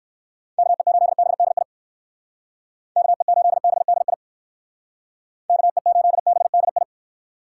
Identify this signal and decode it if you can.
Morse code, heard as a keyed tone, about 59 wpm, 700 Hz. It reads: KE8BDI KE8BDI KE8BDI